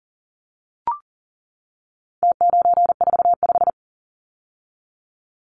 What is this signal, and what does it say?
Morse code, heard as a keyed tone, about 40 wpm, 700 Hz. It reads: T945
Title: Numbers - Contest Style - 4-Digit 40wpm